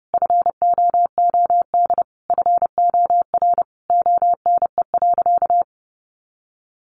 Morse code, heard as a keyed tone, about 30 wpm, 700 Hz. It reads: FOOD FOR ONE.
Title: Sentences from Top 300 Words 30wpm